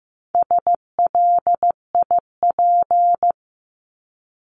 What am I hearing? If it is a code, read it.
Morse code, heard as a keyed tone, about 15 wpm, 700 Hz. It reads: SLIP
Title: Top 1751-2000 Words 15wpm